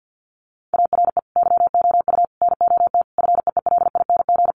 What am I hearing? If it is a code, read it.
Morse code, heard as a keyed tone, about 50 wpm, 700 Hz. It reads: ARE YOU NOT FEELING
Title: Sentences from Top 1200 Words 50wpm